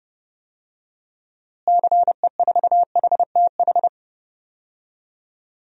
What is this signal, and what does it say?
Morse code, heard as a keyed tone, about 32 wpm, 700 Hz. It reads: CE4HTH
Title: DXCC-100 Call Signs - 6 Characters - Repeated 2x - 30wpm